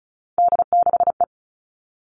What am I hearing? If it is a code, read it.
Morse code, heard as a keyed tone, about 34 wpm, 700 Hz. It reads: D6E
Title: All Cut Numbers - 3-Digit 35wpm